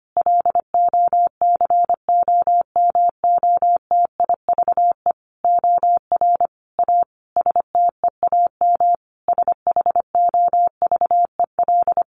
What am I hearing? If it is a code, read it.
Morse code, heard as a keyed tone, about 25 wpm, 700 Hz. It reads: LOCOMOTIVE OR A STEAM SHOVEL
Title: A Sound Of Thunder 25wpm